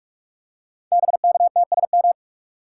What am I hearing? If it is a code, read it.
Morse code, heard as a keyed tone, about 44 wpm, 690 Hz. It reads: BKTSM
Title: Code Groups - 5 Letters - 45wpm